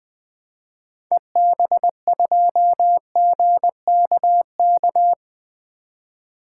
Morse code, heard as a keyed tone, about 21 wpm, 700 Hz. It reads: EB2GKK
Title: 2023 SOTA Activator Call Signs - 20wpm